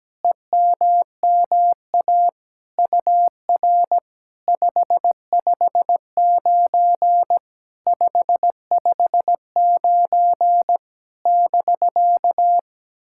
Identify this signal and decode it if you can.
Morse code, heard as a keyed tone, about 17 wpm, 700 Hz. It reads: EMMA UR 559 559 <BK>
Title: SOTA QSO Practice 17wpm